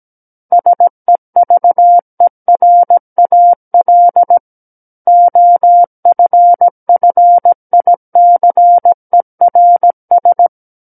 Morse code, heard as a keyed tone, about 17 wpm, 700 Hz. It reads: SEVERAL OFFICERS
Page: SKCC Learning Center - Morse Code Practice Files